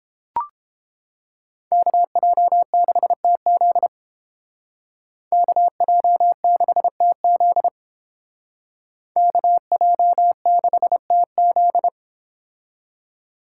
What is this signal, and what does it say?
Morse code, heard as a keyed tone, about 30 wpm, 700 Hz. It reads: KJ6TZ KJ6TZ KJ6TZ